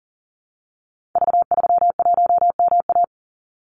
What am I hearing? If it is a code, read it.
Morse code, heard as a keyed tone, about 40 wpm, 700 Hz. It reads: V31MA
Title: Call Signs - 40wpm